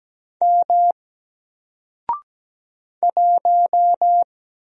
Morse code, heard as a keyed tone, about 17 wpm, 700 Hz. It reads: M 1
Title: Single Letter-Number 17wpm